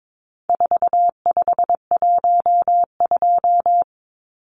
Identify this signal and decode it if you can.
Morse code, heard as a keyed tone, about 22 wpm, 700 Hz. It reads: 4512
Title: Numbers - 4-Digit 22wpm